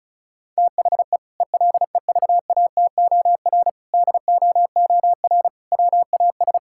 Morse code, heard as a keyed tone, about 36 wpm, 700 Hz. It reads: THE ELEVATOR DOOR WAS
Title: Sentences from Top 1600 Words 35wpm